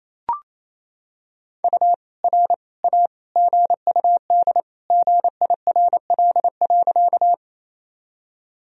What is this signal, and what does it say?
Morse code, heard as a keyed tone, about 27 wpm, 700 Hz. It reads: U R A GUD GIRL.
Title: Sentences from Top 400 Words and Abbreviations 28wpm